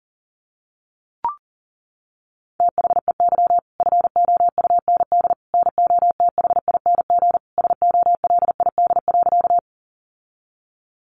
Morse code, heard as a keyed tone, about 40 wpm, 700 Hz. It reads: THEY FOUND NOTHING SOLID.
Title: Sentences from Top 1500 Words 40wpm